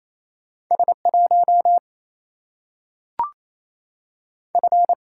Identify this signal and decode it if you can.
Morse code, heard as a keyed tone, about 28 wpm, 700 Hz. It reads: S 1 F
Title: Single-Letter Single-Number 28wpm